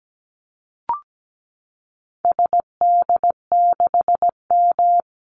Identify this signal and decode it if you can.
Morse code, heard as a keyed tone, about 17 wpm, 700 Hz. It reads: SD6M
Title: Call Signs - 17wpm